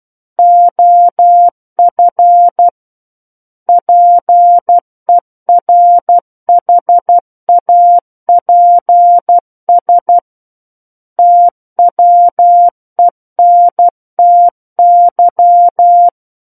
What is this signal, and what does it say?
Morse code, heard as a keyed tone, about 12 wpm, 700 Hz. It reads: OF PERHAPS TWENTY
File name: Book One / 03-Chapter 3 - 12 WPM